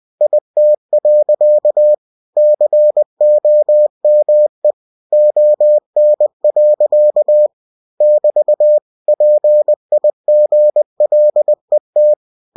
Morse code, heard as a keyed tone, about 20 wpm, 600 Hz. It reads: IT. COME ON. <BT> PIGLET
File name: cw-books / winnie-the-pooh / winnie-20wpm-page-0043